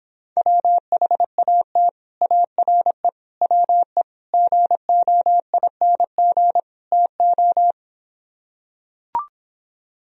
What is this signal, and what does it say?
Morse code, heard as a keyed tone, about 26 wpm, 700 Hz. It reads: WHAT ARE WE GOING TO